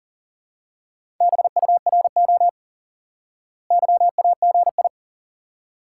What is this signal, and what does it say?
Morse code, heard as a keyed tone, about 40 wpm, 700 Hz. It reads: BURO YAGI
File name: IQE Course 3 - Lesson 89 - YAGI - Two Elements - 40wpm